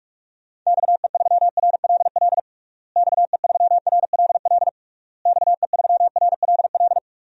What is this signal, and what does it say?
Morse code, heard as a keyed tone, about 46 wpm, 700 Hz. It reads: XE3RLL XE3RLL XE3RLL
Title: DXCC-100 Call Signs - 6 Characters - Repeated 3x - 45wpm